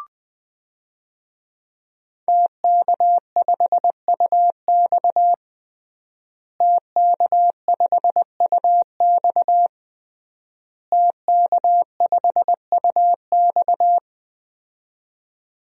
Morse code, heard as a keyed tone, about 20 wpm, 700 Hz. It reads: TK5UX TK5UX TK5UX
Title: DXCC-100 Call Signs - 5 Characters - Repeated 3x - 20wpm